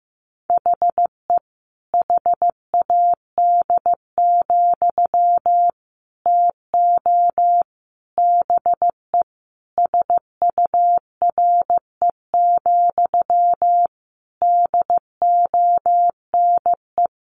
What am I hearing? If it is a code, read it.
Morse code, heard as a keyed tone, about 15 wpm, 700 Hz. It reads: HE HAD, TO BE SURE, DONE
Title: Sentences from Top 500 Words 15wpm